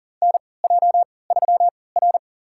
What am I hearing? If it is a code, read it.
Morse code, heard as a keyed tone, about 40 wpm, spaced as for 23 wpm, 700 Hz. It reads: NJ3R
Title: USA Call Signs - 4 Characters - 3x Character Spacing - 40wpm